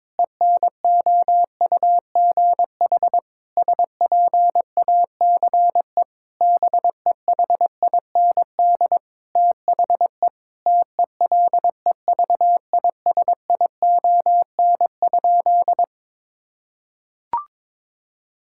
Morse code, heard as a keyed tone, about 22 wpm, 700 Hz. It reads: ENOUGH SPACE BEHIND THE TELEVISION?
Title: Sentences from Top 1000 Words - Review Part 5 - 22wpm